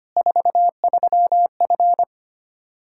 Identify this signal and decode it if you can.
Morse code, heard as a keyed tone, about 25 wpm, 700 Hz. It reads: 43F
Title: K1USN SST QSO Practice 25wpm